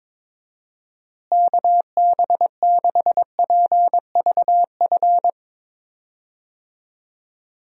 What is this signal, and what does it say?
Morse code, heard as a keyed tone, about 23 wpm, 700 Hz. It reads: KB6PVF
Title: USA Call Signs - 6 Characters - Repeated 2x - 22wpm